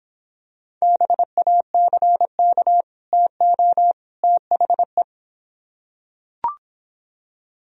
Morse code, heard as a keyed tone, about 24 wpm, 700 Hz. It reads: BACK TO THE